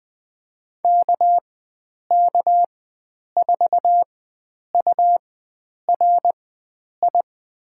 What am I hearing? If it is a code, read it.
Morse code, heard as a keyed tone, about 20 wpm, spaced as for 9 wpm, 700 Hz. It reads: KK4URI
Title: USA Call Signs - 4x Character Spacing - 20wpm